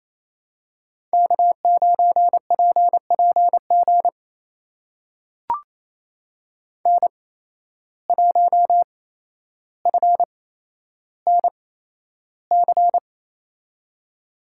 Morse code, heard as a keyed tone, about 28 wpm, spaced as for 12 wpm, 700 Hz. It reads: K9PPG N1FNC
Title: USA Call Signs - 5 Characters - 8x Character Spacing - 28wpm